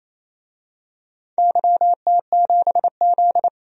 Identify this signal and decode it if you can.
Morse code, heard as a keyed tone, about 28 wpm, 700 Hz. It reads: YT7Z
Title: DXCC-100 Call Signs - 4 Characters - 28wpm